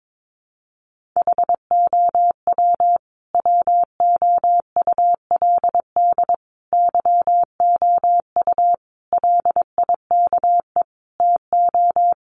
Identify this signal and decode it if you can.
Morse code, heard as a keyed tone, about 22 wpm, 700 Hz. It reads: HOW WOULD YOU LIKE TO
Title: Sentences from Top 100 Words 22wpm